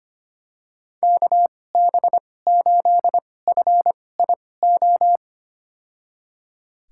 Morse code, heard as a keyed tone, about 25 wpm, spaced as for 18 wpm, 700 Hz. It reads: KB8FIO
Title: USA Call Signs - 6 Characters - 2x Character Spacing - 25wpm